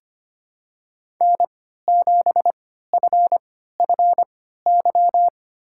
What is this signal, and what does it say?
Morse code, heard as a keyed tone, about 25 wpm, spaced as for 14 wpm, 700 Hz. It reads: N7FFY
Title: USA Call Signs - 5 Characters - 3x Character Spacing - 25wpm